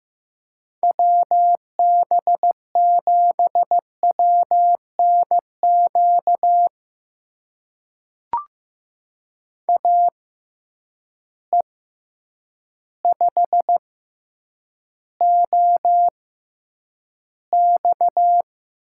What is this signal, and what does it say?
Morse code, heard as a keyed tone, about 15 wpm, spaced as for 8 wpm, 700 Hz. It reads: WB7WNQ AE5OX
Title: USA Call Signs - 6x Character Spacing - 15wpm